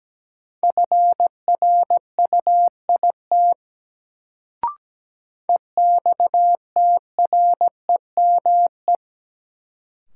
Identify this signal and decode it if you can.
Morse code, heard as a keyed tone, about 17 wpm, 700 Hz. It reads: FRUIT EXTREME
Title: Sentences from Top 1300 Words - New Word Drill - 17wpm